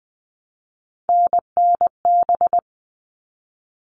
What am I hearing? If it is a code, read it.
Morse code, heard as a keyed tone, about 20 wpm, 700 Hz. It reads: NNB
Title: All Cut Numbers - 3-Digit 20wpm